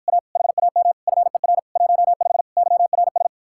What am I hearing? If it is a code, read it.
Morse code, heard as a keyed tone, about 54 wpm, 700 Hz. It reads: A HAM FER 15 YRS